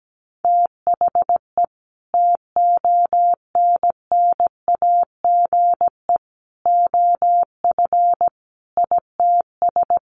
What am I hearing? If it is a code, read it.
Morse code, heard as a keyed tone, about 17 wpm, 700 Hz. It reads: THE TONNAGE OF ITS
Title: A Sound Of Thunder 17wpm